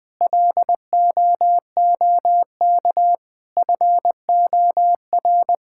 Morse code, heard as a keyed tone, about 20 wpm, 700 Hz. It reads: LOOK FOR
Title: Sentences with 6-Word Phrases - 20wpm